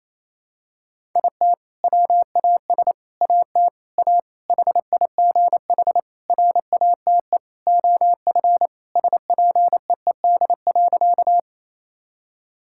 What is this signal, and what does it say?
Morse code, heard as a keyed tone, about 28 wpm, 700 Hz. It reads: IT WAS AT A HIGH RATE OF SPEED.
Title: Sentences from Top 1000 Words - Review Part 5 - 28wpm